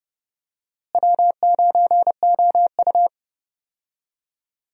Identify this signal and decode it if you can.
Morse code, heard as a keyed tone, about 31 wpm, 700 Hz. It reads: W9OU
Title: USA Call Signs - 4 Characters - Repeated 2x - 30wpm